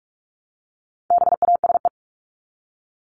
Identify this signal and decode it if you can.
Morse code, heard as a keyed tone, about 44 wpm, 700 Hz. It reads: BASE
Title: Top 500 Words Repeated 45wpm